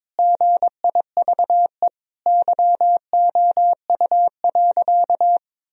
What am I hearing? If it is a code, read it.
Morse code, heard as a keyed tone, about 21 wpm, 700 Hz. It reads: GIVE YOU.
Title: Sentences from Top 1000 Words - Review Part 2 - 22wpm